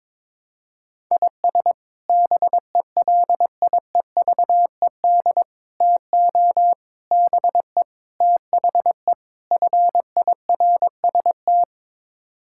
Morse code, heard as a keyed tone, about 22 wpm, 700 Hz. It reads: IS BELIEVED TO BE THE FIRST
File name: practice / 22wpm / Sets of 6 Words - 22wpm